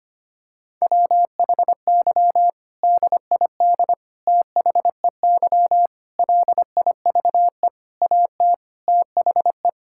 Morse code, heard as a keyed tone, about 25 wpm, 700 Hz. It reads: WHY DID THEY LIVE AT THE
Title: Sentences from Top 1000 Words - Review Part 6 - 25wpm